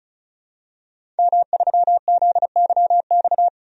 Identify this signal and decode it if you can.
Morse code, heard as a keyed tone, about 36 wpm, 700 Hz. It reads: M3ZYX
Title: DXCC-100 Call Signs - 5 Characters - 35wpm